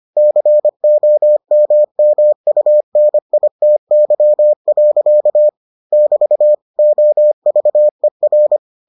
Morse code, heard as a keyed tone, about 25 wpm, 600 Hz. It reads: COMMUNITY. <BT> OVER